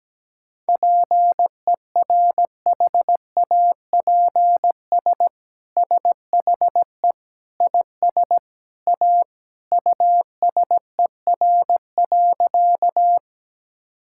Morse code, heard as a keyed tone, about 17 wpm, 700 Hz. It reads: PERHAPS SHE IS A USER.
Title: Sentences from Top 1000 Words - Review Part 5 - 17wpm